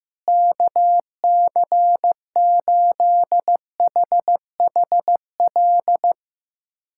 Morse code, heard as a keyed tone, about 15 wpm, 700 Hz. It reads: KC8HHL
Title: USA Callsign DE Callsign 15wpm